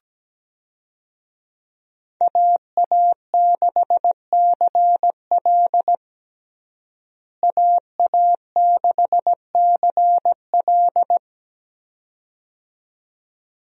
Morse code, heard as a keyed tone, about 17 wpm, 700 Hz. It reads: AA6CL AA6CL
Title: USA Call Signs - 5 Characters - Repeated 2x - 17wpm